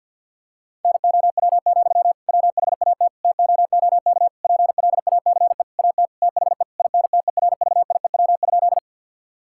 Morse code, heard as a keyed tone, about 50 wpm, 700 Hz. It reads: NOW, WHAT TOOK PLACE AT THE INTERVIEW?
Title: Sentences from Top 1600 Words 50wpm